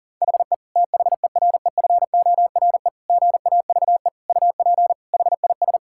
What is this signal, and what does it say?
Morse code, heard as a keyed tone, about 37 wpm, 700 Hz. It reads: HE THEREFORE GAVE UP HIS
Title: Sentences from Top 1000 Words - Review Part 5 - 40wpm